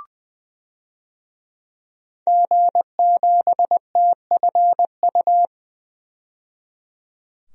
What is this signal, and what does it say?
Morse code, heard as a keyed tone, about 20 wpm, 700 Hz. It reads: G7TFU
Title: DXCC-100 Call Signs - 5 Characters - 20wpm